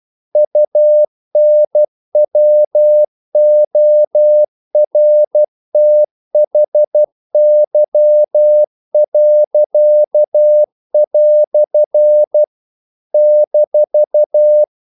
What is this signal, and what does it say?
Morse code, heard as a keyed tone, about 12 wpm, 600 Hz. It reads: UNWORTHY." -
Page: English quotations in CW (telegraphy - morse)